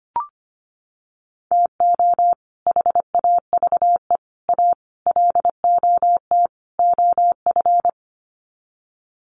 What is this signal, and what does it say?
Morse code, heard as a keyed tone, about 25 wpm, 700 Hz. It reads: TO HAVE A LOT OF
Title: Sets of 5 Words Encore 25wpm